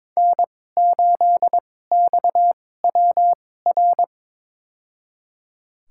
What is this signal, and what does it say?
Morse code, heard as a keyed tone, about 22 wpm, spaced as for 16 wpm, 700 Hz. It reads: N8XWR
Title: USA Call Signs - 5 Characters - 2x Character Spacing - 22wpm